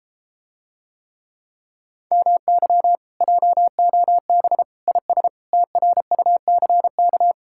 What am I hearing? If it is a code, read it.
Morse code, heard as a keyed tone, about 33 wpm, 700 Hz. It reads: MY JOB IS TRUCK